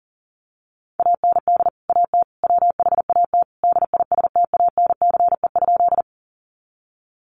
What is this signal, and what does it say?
Morse code, heard as a keyed tone, about 40 wpm, 700 Hz. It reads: AND AT WHAT DISTANCE?
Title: Sentences from Top 1200 Words 40wpm